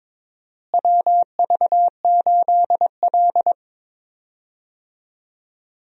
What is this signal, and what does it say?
Morse code, heard as a keyed tone, about 22 wpm, 700 Hz. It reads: WV8L
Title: USA Call Signs - 4 Characters - Repeated 2x - 22wpm